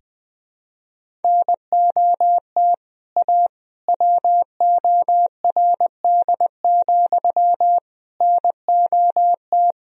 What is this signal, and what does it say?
Morse code, heard as a keyed tone, about 20 wpm, 700 Hz. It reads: NOT A WORD, NOT
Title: Sentences from Top 1000 Words - Review Part 1 - 20wpm